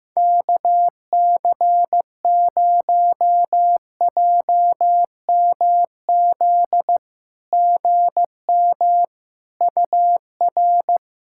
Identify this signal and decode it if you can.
Morse code, heard as a keyed tone, about 16 wpm, 690 Hz. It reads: KC0JMZ GM UR
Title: SOTA QSO Practice 15wpm